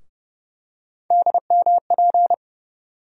Morse code, heard as a keyed tone, about 30 wpm, 700 Hz. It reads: DMP
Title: USA Call Sign Postfixes - 3 Letters 30wpm